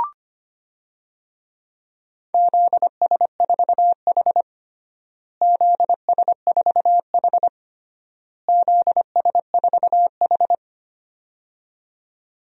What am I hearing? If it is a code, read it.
Morse code, heard as a keyed tone, about 25 wpm, 700 Hz. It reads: ZS4H ZS4H ZS4H
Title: DXCC-100 Call Signs - 4 Characters - Repeated 3x - 25wpm